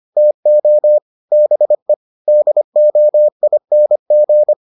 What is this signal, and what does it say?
Morse code, heard as a keyed tone, about 25 wpm, 600 Hz. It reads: TO BE DOING